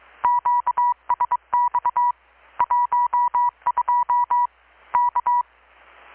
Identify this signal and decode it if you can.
Morse code, heard as a keyed tone, about 22 wpm, 970 Hz. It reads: QSX 12 K
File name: blog / swl / 6-abril-2019-12695-USB-KPS